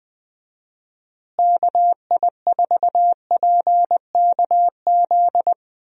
Morse code, heard as a keyed tone, about 21 wpm, 700 Hz. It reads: KI4PKZ
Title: USA Call Signs - 6 Characters - Repeated 3x - 20wpm